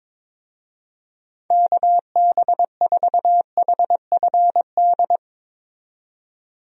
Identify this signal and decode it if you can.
Morse code, heard as a keyed tone, about 22 wpm, 700 Hz. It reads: KB4HFD
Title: USA Call Signs - 6 Characters - 22wpm